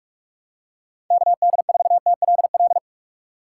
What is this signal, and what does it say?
Morse code, heard as a keyed tone, about 44 wpm, 700 Hz. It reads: KD4TLL
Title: USA Call Signs - 3x Character Spacing - 45wpm